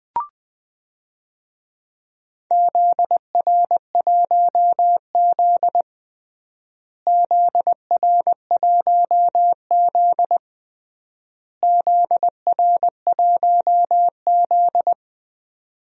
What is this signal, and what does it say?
Morse code, heard as a keyed tone, about 20 wpm, 700 Hz. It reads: ZR1Z ZR1Z ZR1Z
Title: DXCC-100 Call Signs - 4 Characters - Repeated 3x - 20wpm